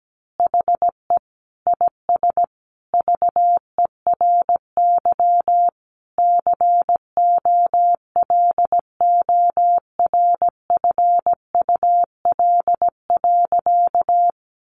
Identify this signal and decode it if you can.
Morse code, heard as a keyed tone, about 17 wpm, 700 Hz. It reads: HE IS VERY COLORFUL.
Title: Easy Phrases 17wpm